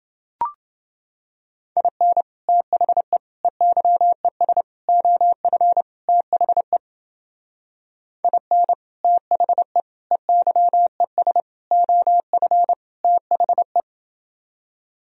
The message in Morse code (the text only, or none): IN THE EYES OF THE IN THE EYES OF THE